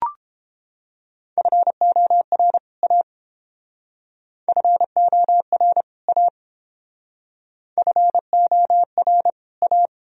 Morse code, heard as a keyed tone, about 29 wpm, 700 Hz. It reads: FOR A FOR A FOR A